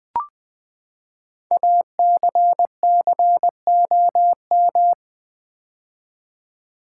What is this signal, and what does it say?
Morse code, heard as a keyed tone, about 20 wpm, 700 Hz. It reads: ACCOM
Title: Build A Word - 5 Letters At A Time - 10-Letter Words - 20wpm